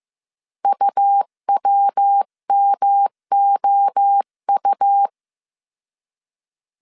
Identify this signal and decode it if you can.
Morse code, heard as a keyed tone, about 15 wpm, 800 Hz. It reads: UWMOU
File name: CW / DET / 1 9 1500C S B 14wpm 3302